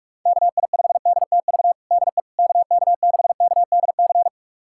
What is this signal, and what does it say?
Morse code, heard as a keyed tone, about 46 wpm, 700 Hz. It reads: KI5DTV DE KK6KDC
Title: USA Callsign DE Callsign - 6 Characters - 45wpm